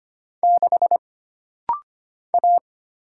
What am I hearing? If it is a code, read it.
Morse code, heard as a keyed tone, about 25 wpm, 700 Hz. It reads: 6 A
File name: Single Character (Letter:Number:Punctuation:Prosign) - Rapid-Fire - 25wpm